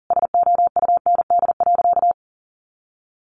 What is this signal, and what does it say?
Morse code, heard as a keyed tone, about 37 wpm, 700 Hz. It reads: SOUND.
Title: A Sound Of Thunder 40wpm